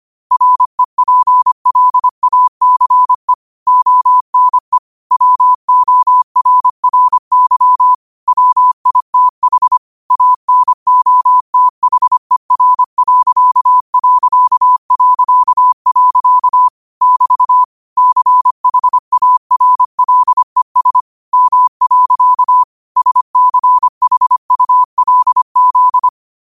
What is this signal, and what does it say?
Morse code, heard as a keyed tone, about 25 wpm, 1000 Hz. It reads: REPLACE ONE WORRY WITH ANOTHER.... <BT> CHARLES M. SCHULZ
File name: qotdfiles / 25wpm